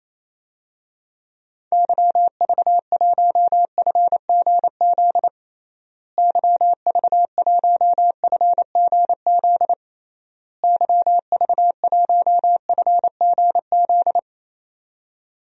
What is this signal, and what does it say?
Morse code, heard as a keyed tone, about 28 wpm, 700 Hz. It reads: YV1FGZ YV1FGZ YV1FGZ
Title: DXCC-100 Call Signs - 6 Characters - Repeated 3x - 28wpm